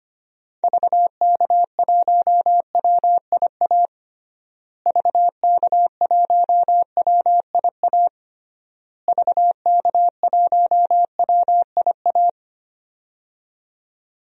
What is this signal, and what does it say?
Morse code, heard as a keyed tone, about 25 wpm, 700 Hz. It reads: VK1WIA VK1WIA VK1WIA
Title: DXCC-100 Call Signs - 6 Characters - Repeated 3x - 25wpm